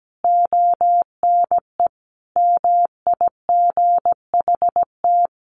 Morse code, heard as a keyed tone, about 17 wpm, 700 Hz. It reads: ONE MIGHT
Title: Sentences from Top 1500 Words 17wpm